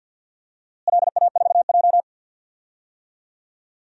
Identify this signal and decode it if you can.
Morse code, heard as a keyed tone, about 51 wpm, 700 Hz. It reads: LA4J
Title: DXCC-100 Call Signs - 4 Characters - Repeated 2x - 50wpm